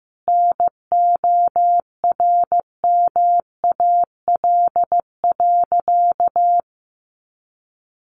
Morse code, heard as a keyed tone, about 15 wpm, 700 Hz. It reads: NORMAL.
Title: Sentences from Top 1200 Words 15wpm